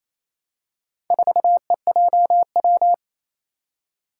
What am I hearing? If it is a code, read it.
Morse code, heard as a keyed tone, about 27 wpm, 700 Hz. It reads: 4EJW
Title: Four Challenging Letter-Number Combinations 28wpm